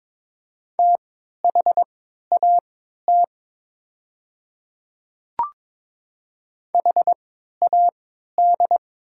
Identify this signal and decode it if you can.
Morse code, heard as a keyed tone, about 21 wpm, spaced as for 12 wpm, 700 Hz. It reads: THAT HAD
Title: 3x Character Spacing - Top 100 Words - 22wpm